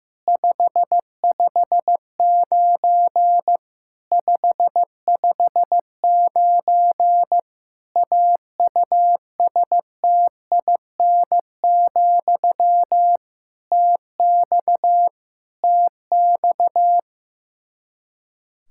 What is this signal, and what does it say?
Morse code, heard as a keyed tone, about 15 wpm, 700 Hz. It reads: 559 559 AUSTIN, TX TX
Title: K1USN SST QSO Practice 15wpm